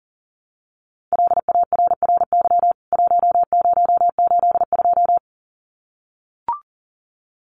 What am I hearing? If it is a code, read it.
Morse code, heard as a keyed tone, about 40 wpm, 700 Hz. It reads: LARRY 1082
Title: CWT Exchange 40wpm